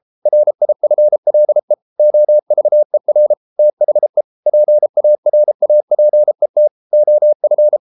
Morse code, heard as a keyed tone, about 33 wpm, 600 Hz. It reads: RIFLE OVER THE PARAPET OF